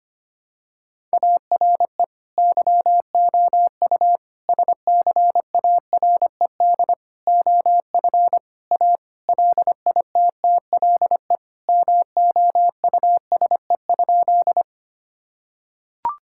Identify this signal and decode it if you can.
Morse code, heard as a keyed tone, about 25 wpm, 700 Hz. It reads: ARE YOU SCARED OF A LITTLE MOUSE?
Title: Sentences from Top 1000 Words - Review Part 5 - 25wpm